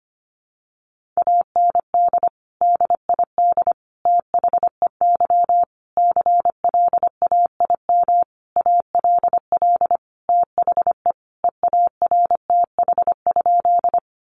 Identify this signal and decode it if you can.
Morse code, heard as a keyed tone, about 25 wpm, 700 Hz. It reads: AND DID THEY CLAIM ALL THE EARTH?
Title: Sentences from Top 1500 Words 25wpm